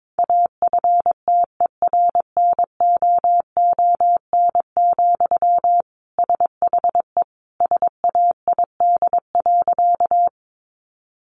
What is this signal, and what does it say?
Morse code, heard as a keyed tone, about 22 wpm, 700 Hz. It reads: AFTERNOON, SHE SAID.
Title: Sentences from Top 1500 Words 22wpm